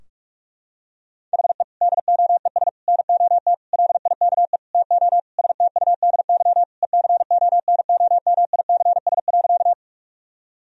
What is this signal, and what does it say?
Morse code, heard as a keyed tone, about 46 wpm, 700 Hz. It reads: HE DOES NOT LIKE TO STUDY ECONOMICS.
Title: Sentences from Top 1000 Words - Review Part 3 - 45wpm